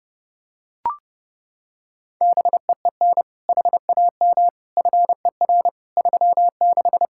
Morse code, heard as a keyed tone, about 31 wpm, 700 Hz. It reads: BEEN HAM FER 36
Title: Practice QSO - Exchange 3 - More Info - 30wpm